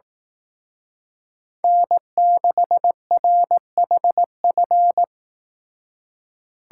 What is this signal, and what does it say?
Morse code, heard as a keyed tone, about 18 wpm, 700 Hz. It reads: N6RHF
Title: Speed Racing - USA 1x3 Call Signs 15wpm